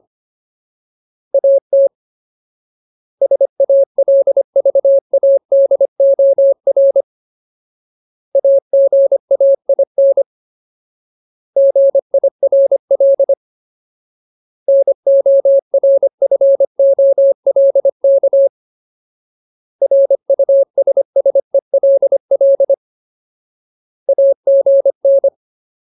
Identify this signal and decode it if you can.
Morse code, heard as a keyed tone, about 25 wpm, 550 Hz. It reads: AT SALVADOR AGAIN GIRL NORFOLK RUSSELL AGN
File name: Ultimate Word List / 25WPM Ultimate